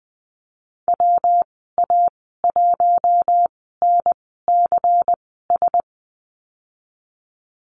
Morse code, heard as a keyed tone, about 20 wpm, spaced as for 14 wpm, 700 Hz. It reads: WA1NCS
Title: 122 Lesson 20g - Number 3 - Call Signs 20wpm